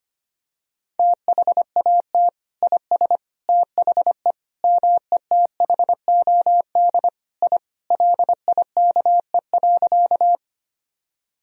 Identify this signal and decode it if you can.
Morse code, heard as a keyed tone, about 25 wpm, 700 Hz. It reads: THAT IS THE METHOD I LIKE.
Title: Sentences from Top 1000 Words - Review Part 3 - 25wpm